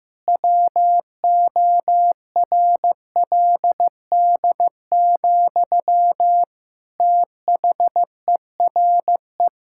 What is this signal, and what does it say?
Morse code, heard as a keyed tone, about 16 wpm, 700 Hz. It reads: WORLD, THERE
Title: Sentences from Top 1000 Words - Review Part 5 - 15wpm